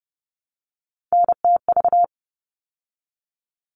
Morse code, heard as a keyed tone, about 30 wpm, 700 Hz. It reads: NTV
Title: All Cut Numbers - 3-Digit 30wpm